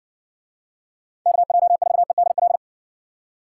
Morse code, heard as a keyed tone, about 54 wpm, 700 Hz. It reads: KJ4EDL